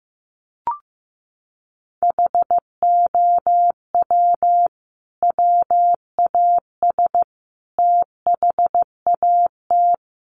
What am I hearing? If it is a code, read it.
Morse code, heard as a keyed tone, about 15 wpm, 700 Hz. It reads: HOW WAS THAT
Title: Sentences from Top 500 Words 15wpm